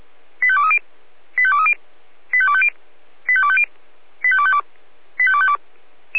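Five-tone selective call (ZVEI), heard as quick runs of five tones, tones of about 80 ms. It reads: 85230 85230 85250 85250 85252 85252
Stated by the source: ZVEI five-tone selective call received on an SDR.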